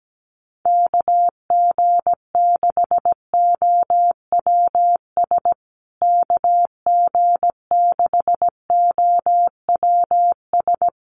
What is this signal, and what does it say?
Morse code, heard as a keyed tone, about 17 wpm, 700 Hz. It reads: KG6OWS KG6OWS
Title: USA Call Signs Repeated 3x 17wpm